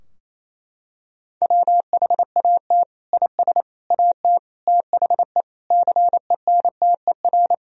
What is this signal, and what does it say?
Morse code, heard as a keyed tone, about 27 wpm, 700 Hz. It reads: WHAT IS AT THE CENTER
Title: Sentences from Top 1000 Words - Review Part 2 - 28wpm